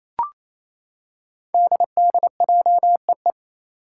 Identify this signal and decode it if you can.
Morse code, heard as a keyed tone, about 27 wpm, 700 Hz. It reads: DDJEE
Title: Five Challenging Letter-Number Combinations 28wpm